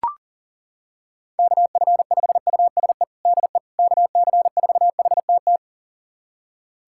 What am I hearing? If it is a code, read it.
Morse code, heard as a keyed tone, about 40 wpm, 700 Hz. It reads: KF5USE DE KC4HTT
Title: USA Callsign DE Callsign 40wpm